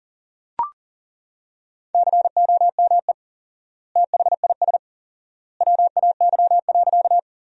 Morse code, heard as a keyed tone, about 40 wpm, 700 Hz. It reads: COME THIS WAY.
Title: 4x Word-Spacing - Sentences from Top 100 Words - 40wpm